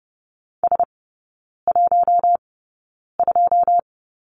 Morse code, heard as a keyed tone, about 30 wpm, spaced as for 9 wpm, 700 Hz. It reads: S12
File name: 081 Lesson 14e - Number 2 - Three Characters 30wpm